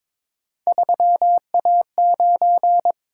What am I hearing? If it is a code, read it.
Morse code, heard as a keyed tone, about 22 wpm, 700 Hz. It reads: 3A9
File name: practice / 22wpm / Three Letter-Number Combinations - 22wpm